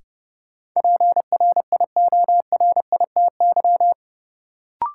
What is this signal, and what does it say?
Morse code, heard as a keyed tone, about 30 wpm, 700 Hz. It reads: PRIORITY
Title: Sentences from Top 1600 Words - New Word Drill Alt - 30wpm